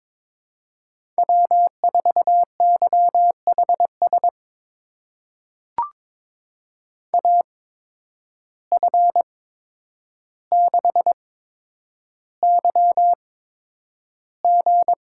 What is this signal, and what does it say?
Morse code, heard as a keyed tone, about 22 wpm, spaced as for 9 wpm, 700 Hz. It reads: W4YHS AF6YG
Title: USA Call Signs - 5 Characters - 8x Character Spacing - 22wpm